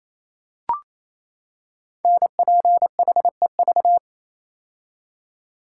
Morse code, heard as a keyed tone, about 29 wpm, 700 Hz. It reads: NPHEV
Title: Code Groups - 5 Letters - 28wpm